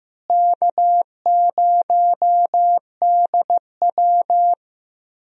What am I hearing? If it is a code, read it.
Morse code, heard as a keyed tone, about 15 wpm, 700 Hz. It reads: K0DW
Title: USA Call Signs - 4 Characters - Repeated 3x - 15wpm